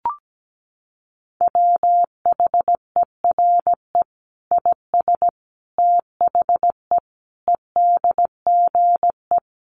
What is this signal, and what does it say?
Morse code, heard as a keyed tone, about 17 wpm, 700 Hz. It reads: WHERE IS THE EDGE
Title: Sentences from Top 1300 Words 17wpm